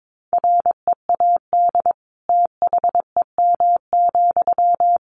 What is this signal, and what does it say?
Morse code, heard as a keyed tone, about 22 wpm, 700 Hz. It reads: READ THEM,
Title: Sentences from Top 400 Words 22wpm